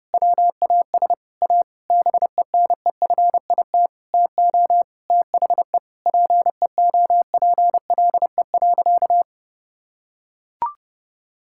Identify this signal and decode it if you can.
Morse code, heard as a keyed tone, about 32 wpm, 700 Hz. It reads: WAS A BENEFIT TO THE PEOPLE.
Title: Sentences from Top 1000 Words - Review Part 6 - 30wpm